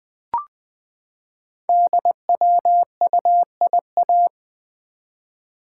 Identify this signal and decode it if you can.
Morse code, heard as a keyed tone, about 20 wpm, 700 Hz. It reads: DWUIA
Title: Code Groups - 5 Letter:Numbers - 20wpm